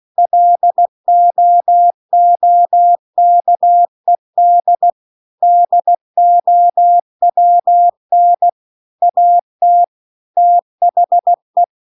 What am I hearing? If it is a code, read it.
Morse code, heard as a keyed tone, about 16 wpm, 700 Hz. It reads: LOOKED DOWN AT THE
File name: CW-The-Big-Twoheaded-River-2-16WPM